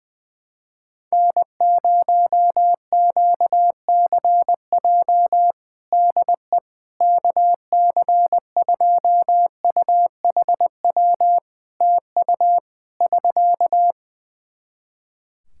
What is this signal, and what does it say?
Morse code, heard as a keyed tone, about 20 wpm, 700 Hz. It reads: N0QCJ DE KC2UHW TU <SK>
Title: K1USN SST QSO Practice 20wpm